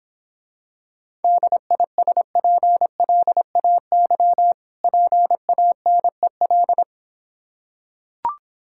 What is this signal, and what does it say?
Morse code, heard as a keyed tone, about 26 wpm, 700 Hz. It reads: DISPLAY PANEL